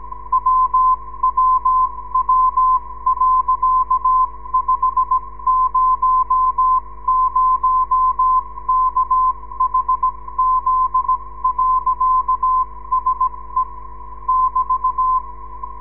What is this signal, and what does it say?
Morse code, heard as a keyed tone, about 17 wpm, spaced as for 15 wpm, 1030 Hz, with radio noise. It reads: WWW.500KHZ.SE <BT>
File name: ndb zvok / sm6bgp